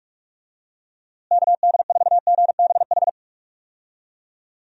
Morse code, heard as a keyed tone, about 44 wpm, 700 Hz. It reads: KD4GBH
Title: USA Call Signs - 6 Characters - Repeated 2x - 45wpm